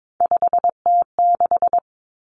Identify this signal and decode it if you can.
Morse code, heard as a keyed tone, about 22 wpm, 700 Hz. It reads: 5T6
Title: Common Cut Numbers - 3-Digit 22wpm